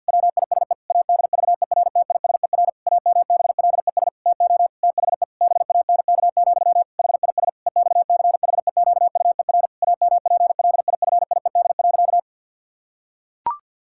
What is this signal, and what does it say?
Morse code, heard as a keyed tone, about 50 wpm, 700 Hz. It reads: WISE ADVERTISER AMBLES TO THE BANK, HIS EXCHEQUER AMPLIFIED.
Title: Pangram Challenge v2 - 50wpm